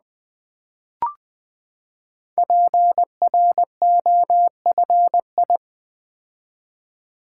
Morse code, heard as a keyed tone, about 20 wpm, 700 Hz. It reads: PROFI
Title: Build A Word - 5 Letters At A Time - 10-Letter Words - 20wpm